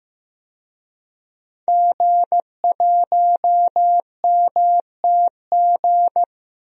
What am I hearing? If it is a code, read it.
Morse code, heard as a keyed tone, about 15 wpm, 700 Hz. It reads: G1MTG
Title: DXCC-100 Call Signs - 5 Characters - Repeated 2x - 15wpm